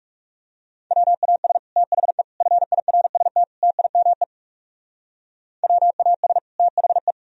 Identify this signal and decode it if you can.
Morse code, heard as a keyed tone, about 38 wpm, 700 Hz. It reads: WAS THE FIRST TIME WAS THE